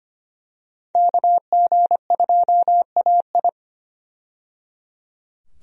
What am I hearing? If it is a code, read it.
Morse code, heard as a keyed tone, about 25 wpm, 700 Hz. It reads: KG2AI
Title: USA Call Signs - 5 Characters - 25wpm